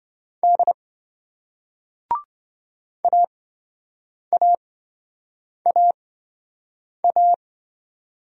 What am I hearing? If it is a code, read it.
Morse code, heard as a keyed tone, about 28 wpm, spaced as for 7 wpm, 700 Hz. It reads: D AAAA